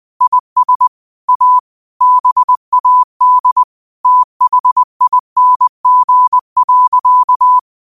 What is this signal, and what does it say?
Morse code, heard as a keyed tone, about 20 wpm, 1000 Hz. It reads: IS A BAD THING.